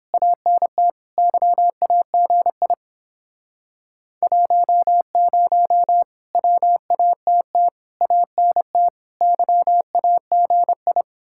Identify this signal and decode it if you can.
Morse code, heard as a keyed tone, about 31 wpm, 700 Hz. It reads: ANT YAGI 10 WATT ANT YAGI